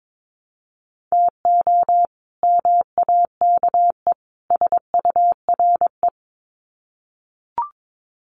Morse code, heard as a keyed tone, about 22 wpm, 700 Hz. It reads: TO MAKE SURE
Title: Sets of 3 Words 22wpm